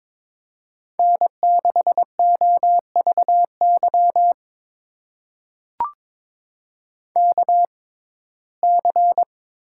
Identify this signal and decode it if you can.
Morse code, heard as a keyed tone, about 22 wpm, spaced as for 16 wpm, 700 Hz. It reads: N6OVY KC
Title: USA Call Signs - 6x Character Spacing - 22wpm